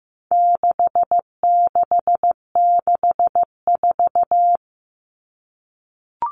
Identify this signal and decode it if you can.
Morse code, heard as a keyed tone, about 15 wpm, 700 Hz. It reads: 6664
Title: Numbers - 4-Digit 15wpm